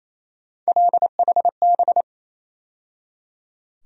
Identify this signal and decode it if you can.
Morse code, heard as a keyed tone, about 28 wpm, 700 Hz. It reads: LHB
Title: USA Call Sign Postfixes - 3 Letters 28wpm